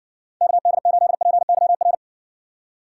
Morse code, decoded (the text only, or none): KD8PYR